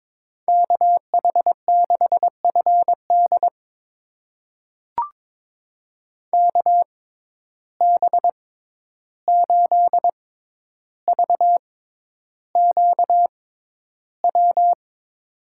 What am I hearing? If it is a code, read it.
Morse code, heard as a keyed tone, about 22 wpm, spaced as for 11 wpm, 700 Hz. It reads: KH6FD KB8VQW
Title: USA Call Signs - 6x Character Spacing - 22wpm